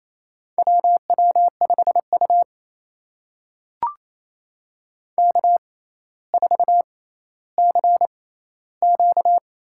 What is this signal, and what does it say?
Morse code, heard as a keyed tone, about 28 wpm, spaced as for 14 wpm, 700 Hz. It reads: WW5U K4CQ
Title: USA Call Signs - 4 Characters - 6x Character Spacing - 28wpm